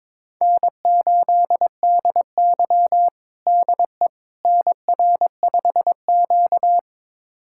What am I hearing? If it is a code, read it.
Morse code, heard as a keyed tone, about 22 wpm, 700 Hz. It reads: N8DY DE NR5Q
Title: USA Callsign DE Callsign - 4 Characters - 22wpm